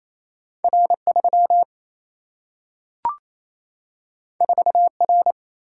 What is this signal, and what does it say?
Morse code, heard as a keyed tone, about 28 wpm, 700 Hz. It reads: R3 4R
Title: Single-Letter Single-Number 28wpm